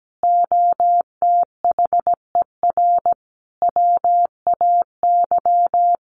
Morse code, heard as a keyed tone, about 17 wpm, 700 Hz. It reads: OTHER WAY
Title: Sentences from Top 200 Words 17wpm